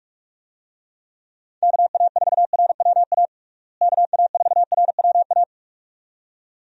Morse code, heard as a keyed tone, about 42 wpm, 700 Hz. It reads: KA4RWA KA4RWA
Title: USA Call Signs - 6 Characters - Repeated 2x - 45wpm